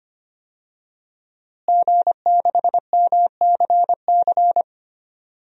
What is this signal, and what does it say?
Morse code, heard as a keyed tone, about 25 wpm, 700 Hz. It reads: G6MCC
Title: DXCC-100 Call Signs - 5 Characters - Repeated 2x - 25wpm